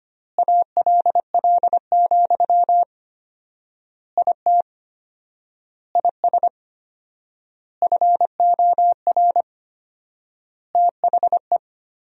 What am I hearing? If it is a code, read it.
Morse code, heard as a keyed tone, about 25 wpm, 700 Hz. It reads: ALL, IT IS FOR THE
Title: 4x Word-Spacing - Sentences from Top 500 Words - 25wpm